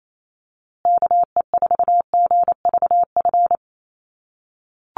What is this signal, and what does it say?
Morse code, heard as a keyed tone, about 28 wpm, 700 Hz. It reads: KE4GVF
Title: USA Call Signs Repeated 3x 28wpm